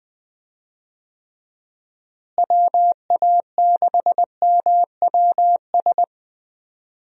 Morse code, heard as a keyed tone, about 20 wpm, 700 Hz. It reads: WA6MWS
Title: USA Call Signs - 6 Characters - Repeated 3x - 20wpm